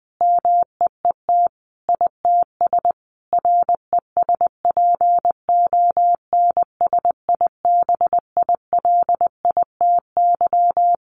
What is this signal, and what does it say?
Morse code, heard as a keyed tone, about 20 wpm, 700 Hz. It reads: MEET ITS RESPONSIBILITY
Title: Sentences from Top 1200 Words 20wpm